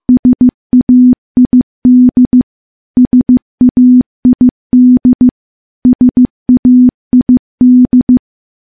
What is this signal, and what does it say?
Morse code, heard as a keyed tone, about 15 wpm, 250 Hz. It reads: SAID SAID SAID
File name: morse / morse-training-15WPM